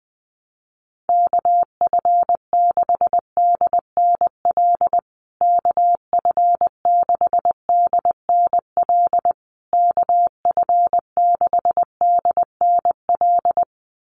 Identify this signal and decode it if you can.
Morse code, heard as a keyed tone, about 20 wpm, 700 Hz. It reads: KF6DNL KF6DNL KF6DNL
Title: USA Call Signs Repeated 3x 20wpm